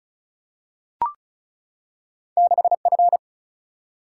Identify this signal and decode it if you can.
Morse code, heard as a keyed tone, about 36 wpm, 700 Hz. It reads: 6F
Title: Single-Letter Single-Number 35wpm